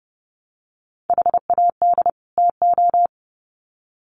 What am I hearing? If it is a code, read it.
Morse code, heard as a keyed tone, about 30 wpm, 700 Hz. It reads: HAD TO
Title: Sets of 2 Words 30wpm